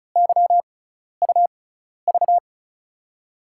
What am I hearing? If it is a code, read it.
Morse code, heard as a keyed tone, about 36 wpm, spaced as for 12 wpm, 700 Hz. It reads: YUV
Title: USA Call Signs - 5 Characters - 6x Character Spacing - 35wpm